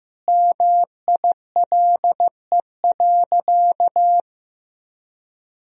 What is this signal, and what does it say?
Morse code, heard as a keyed tone, about 14 wpm, 700 Hz. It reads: MILE.
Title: Sentences from Top 1000 Words - Review Part 1 - 15wpm